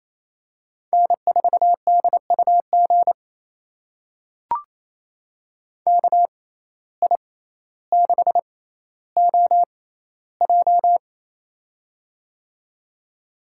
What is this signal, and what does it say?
Morse code, heard as a keyed tone, about 28 wpm, spaced as for 14 wpm, 700 Hz. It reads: N4DUG KI6OJ
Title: USA Call Signs - 5 Characters - 6x Character Spacing - 28wpm